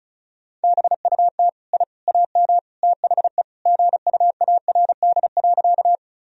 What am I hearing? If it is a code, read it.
Morse code, heard as a keyed tone, about 36 wpm, 700 Hz. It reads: BUT I AM THE GUARD.
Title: Sentences from Top 1000 Words - Review Part 4 - 35wpm